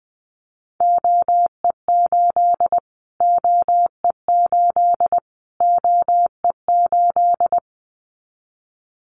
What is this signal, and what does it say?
Morse code, heard as a keyed tone, about 20 wpm, 700 Hz. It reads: OE8 OE8 OE8
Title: Callsign Prefixes for Top 25 DXCC - Repeated 3x - 20wpm